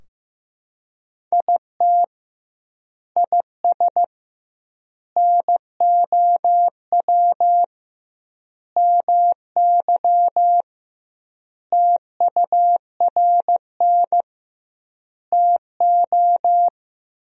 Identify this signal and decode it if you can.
Morse code, heard as a keyed tone, about 15 wpm, 700 Hz. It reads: IT IS NOW MY TURN TO
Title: 2x Word-Spacing - Sentences from Top 200 Words - 15wpm